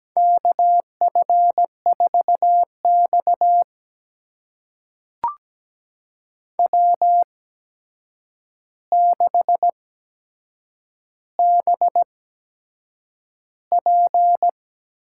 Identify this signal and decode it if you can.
Morse code, heard as a keyed tone, about 17 wpm, spaced as for 7 wpm, 700 Hz. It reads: KF4X W6BP
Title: USA Call Signs - 4 Characters - 8x Character Spacing - 17wpm